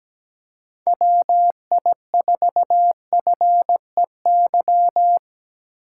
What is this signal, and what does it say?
Morse code, heard as a keyed tone, about 17 wpm, 700 Hz. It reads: WI4FEY
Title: USA Call Signs - 6 Characters - 6x Character Spacing - 17wpm